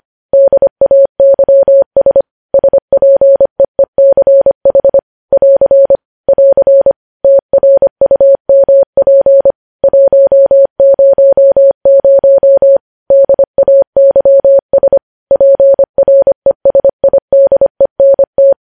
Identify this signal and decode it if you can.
Morse code, heard as a keyed tone, about 25 wpm, 550 Hz. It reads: DAYS SPEECH <AR> <AR> TRUMP 100 DAYS PRESIDENT
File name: podcast / News headlines-1x-25x25